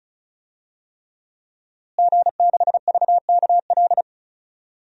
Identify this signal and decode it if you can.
Morse code, heard as a keyed tone, about 36 wpm, 700 Hz. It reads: G6VKL
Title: DXCC-100 Call Signs - 5 Characters - Repeated 3x - 35wpm